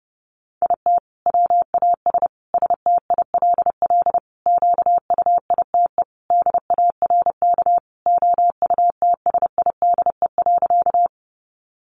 Morse code, heard as a keyed tone, about 30 wpm, 700 Hz. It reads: IT WAS STILL QUITE DARK OUTSIDE.
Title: Sentences from Top 900 Words 30wpm